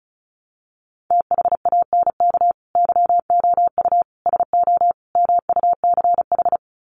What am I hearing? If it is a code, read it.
Morse code, heard as a keyed tone, about 34 wpm, 700 Hz. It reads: THANK YOU SO MUCH
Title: Sets of 4 Words 35wpm